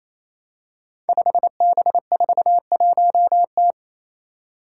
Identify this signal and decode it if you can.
Morse code, heard as a keyed tone, about 27 wpm, 700 Hz. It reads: 5B41T
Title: Five Challenging Letter-Number Combinations 28wpm